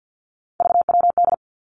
Morse code, heard as a keyed tone, about 50 wpm, 700 Hz. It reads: VWD
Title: Three Challenging Letter-Number Combinations 50wpm